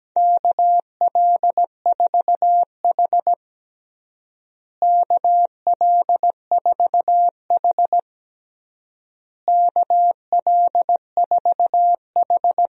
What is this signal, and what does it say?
Morse code, heard as a keyed tone, about 17 wpm, 700 Hz. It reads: KL4H KL4H KL4H
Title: USA Call Signs - 4 Characters - Repeated 3x - 17wpm